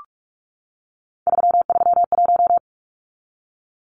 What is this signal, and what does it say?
Morse code, heard as a keyed tone, about 44 wpm, 700 Hz. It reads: 331
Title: Numbers - 3-Digit 45wpm